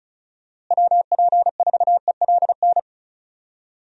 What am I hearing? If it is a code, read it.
Morse code, heard as a keyed tone, about 34 wpm, 700 Hz. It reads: WP4ELN
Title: USA Call Signs - 6 Characters - Repeated 3x - 35wpm